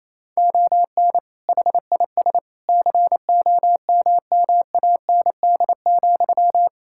Morse code, heard as a keyed tone, about 28 wpm, 700 Hz. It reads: ON HIS COMMAND,
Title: Sentences from Top 1600 Words 28wpm